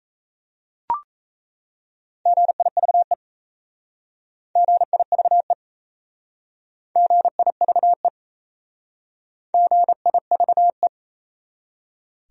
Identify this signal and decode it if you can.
Morse code, heard as a keyed tone, about 36 wpm, 700 Hz. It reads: GIVE GIVE GIVE GIVE